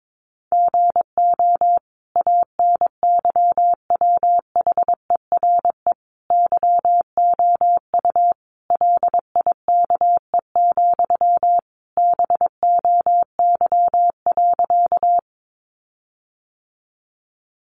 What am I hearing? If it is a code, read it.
Morse code, heard as a keyed tone, about 22 wpm, 700 Hz. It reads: GO ANYWHERE YOU LIKE, BOY.
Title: Sentences from Top 1400 Words 22wpm